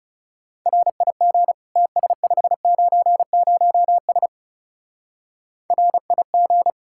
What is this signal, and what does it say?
Morse code, heard as a keyed tone, about 37 wpm, 700 Hz. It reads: RIG TS590S RIG
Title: Speed-Racing Common QSO Phrases Encore - 30wpm